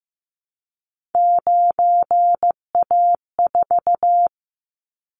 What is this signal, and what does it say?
Morse code, heard as a keyed tone, about 15 wpm, 700 Hz. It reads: 9A4
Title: Callsign Prefixes for Top 100 DXCC - 15wpm